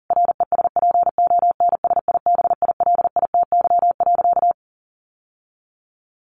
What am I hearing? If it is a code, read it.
Morse code, heard as a keyed tone, about 40 wpm, 700 Hz. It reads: RESPONSIBILITY.
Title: Sentences from Top 1500 Words 40wpm